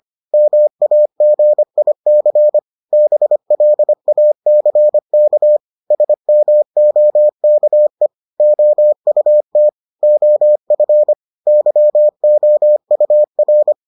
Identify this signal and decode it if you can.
Morse code, heard as a keyed tone, about 25 wpm, 600 Hz. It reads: MAGIC BLACK SMOKE OUT OF YOUR